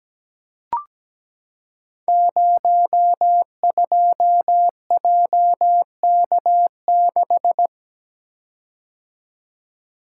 Morse code, heard as a keyed tone, about 17 wpm, 700 Hz. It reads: 02JK6
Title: Code Groups - 5 Letter:Numbers - 17wpm